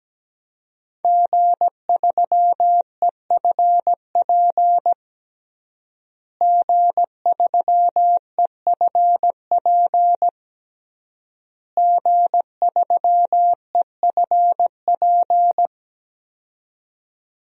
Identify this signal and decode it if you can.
Morse code, heard as a keyed tone, about 17 wpm, 700 Hz. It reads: G3EFP G3EFP G3EFP
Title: DXCC-100 Call Signs - 5 Characters - Repeated 3x - 17wpm